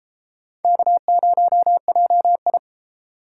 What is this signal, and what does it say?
Morse code, heard as a keyed tone, about 31 wpm, 700 Hz. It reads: K0JI